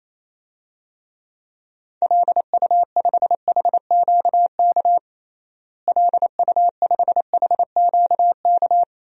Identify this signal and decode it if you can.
Morse code, heard as a keyed tone, about 28 wpm, 700 Hz. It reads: LU5HQK LU5HQK
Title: DXCC-100 Call Signs - 6 Characters - Repeated 2x - 28wpm